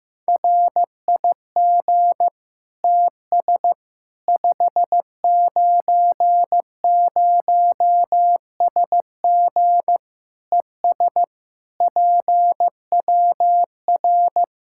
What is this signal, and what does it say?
Morse code, heard as a keyed tone, about 15 wpm, 700 Hz. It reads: RIG TS 590SG ES PWR
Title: Practice QSO - Exchange 2 - Info - 15wpm